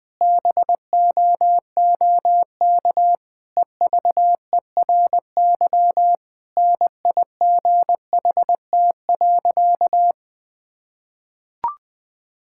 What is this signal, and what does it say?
Morse code, heard as a keyed tone, about 20 wpm, 700 Hz. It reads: BOOK EVERY NIGHT.
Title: Difficult Sentences with Hints - 20wpm